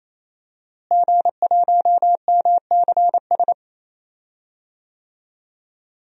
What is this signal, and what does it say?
Morse code, heard as a keyed tone, about 28 wpm, 700 Hz. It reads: G1MCS
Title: DXCC-100 Call Signs - 5 Characters - Repeated 3x - 28wpm